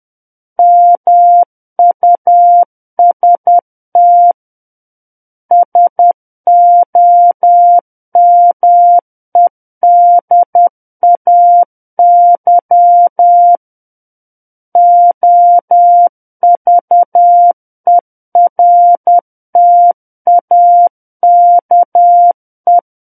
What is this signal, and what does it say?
Morse code, heard as a keyed tone, about 10 wpm, 700 Hz. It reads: MUST SOMEDAY OVERTAKE
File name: Book One / 01-Chapter 1 - 10 WPM